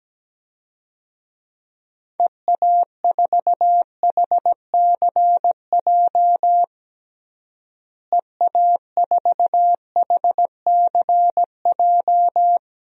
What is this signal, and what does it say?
Morse code, heard as a keyed tone, about 17 wpm, 700 Hz. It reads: EA4HCJ EA4HCJ
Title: DXCC-100 Call Signs - 6 Characters - Repeated 2x - 17wpm